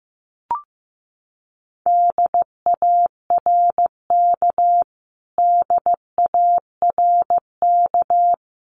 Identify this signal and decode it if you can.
Morse code, heard as a keyed tone, about 15 wpm, 700 Hz. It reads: DARK DARK
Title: Top 500 Words Repeated 15wpm